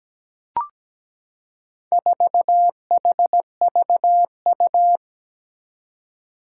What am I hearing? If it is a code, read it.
Morse code, heard as a keyed tone, about 17 wpm, 700 Hz. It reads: 4HVU
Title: Four Challenging Letter-Number Combinations 17wpm